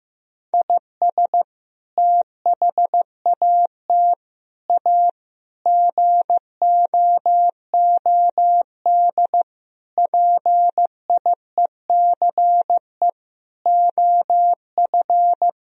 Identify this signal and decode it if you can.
Morse code, heard as a keyed tone, about 15 wpm, 700 Hz. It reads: IS THAT A GOOD PIECE OF
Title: Sentences from Top 1000 Words - Review Part 3 - 15wpm